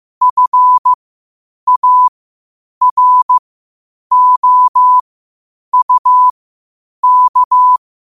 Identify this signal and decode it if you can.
Morse code, heard as a keyed tone, about 15 wpm, spaced as for 8 wpm, 1000 Hz. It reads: FAROUK
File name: qotdfiles / 8wpm